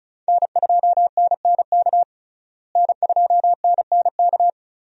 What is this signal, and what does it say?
Morse code, heard as a keyed tone, about 34 wpm, 700 Hz. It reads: N2NNK N2NNK
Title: USA Call Signs - 5 Characters - Repeated 2x - 35wpm